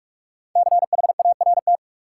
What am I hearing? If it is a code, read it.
Morse code, heard as a keyed tone, about 46 wpm, 700 Hz. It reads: CHART
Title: Sentences from Top 1300 Words - New Word Drill - 45wpm